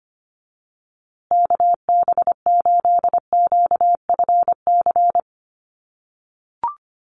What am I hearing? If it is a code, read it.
Morse code, heard as a keyed tone, about 25 wpm, 700 Hz. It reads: KB8QFC
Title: USA Call Signs Repeated 2x 25wpm